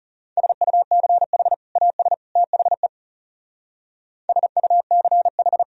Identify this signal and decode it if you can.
Morse code, heard as a keyed tone, about 34 wpm, 700 Hz. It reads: SUCH AS THE SUCH
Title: Speed-Racing - Sets of 3 Words 30wpm